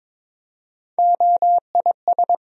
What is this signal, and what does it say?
Morse code, heard as a keyed tone, about 23 wpm, 700 Hz. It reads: OIS
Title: USA Call Sign Postfixes - 3 Letters 22wpm